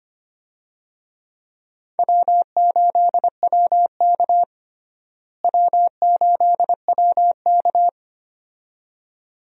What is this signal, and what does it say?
Morse code, heard as a keyed tone, about 25 wpm, 700 Hz. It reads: W8WK W8WK
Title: USA Call Signs - 4 Characters - Repeated 2x - 25wpm